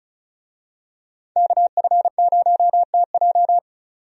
Morse code, heard as a keyed tone, about 36 wpm, 700 Hz. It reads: KF0TJ
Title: USA Call Signs - 5 Characters - 35wpm